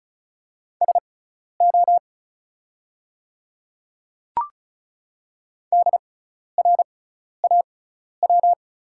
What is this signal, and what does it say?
Morse code, heard as a keyed tone, about 36 wpm, spaced as for 12 wpm, 690 Hz. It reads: SO DRAW
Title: 6x Character Spacing - Top 300 Words - 35wpm